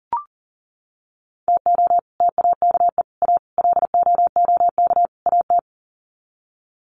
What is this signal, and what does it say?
Morse code, heard as a keyed tone, about 38 wpm, 700 Hz. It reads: TO TAKE A LOOK AT
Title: Sets of 5 Words 40wpm